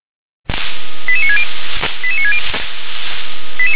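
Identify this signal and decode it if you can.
Five-tone selective call (ZVEI), heard as quick runs of five tones, tones of about 70 ms. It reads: 88960 88960
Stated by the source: ZVEI five-tone selective call 88960 received by SDR.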